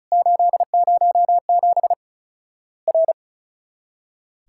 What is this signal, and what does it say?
Morse code, heard as a keyed tone, about 36 wpm, 700 Hz. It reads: 807 R
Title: Worked All Europe DX Contest - QTC Exchange - 35wpm